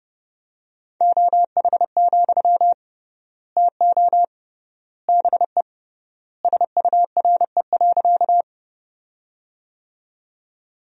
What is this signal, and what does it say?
Morse code, heard as a keyed tone, about 30 wpm, 700 Hz. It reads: OH, TO BE SURE.
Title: 3x Word-Spacing - Sentences from Top 500 Words - 30wpm